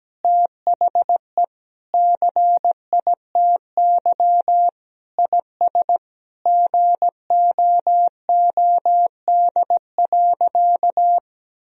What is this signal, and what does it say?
Morse code, heard as a keyed tone, about 17 wpm, 700 Hz. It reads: THE CITY IS GOOD.
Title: Sentences from Top 1000 Words - Review Part 4 - 17wpm